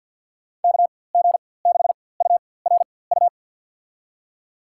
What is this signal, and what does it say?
Morse code, heard as a keyed tone, about 48 wpm, spaced as for 23 wpm, 700 Hz. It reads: KG6URU
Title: USA Call Signs - 6 Characters - 4x Character Spacing - 50wpm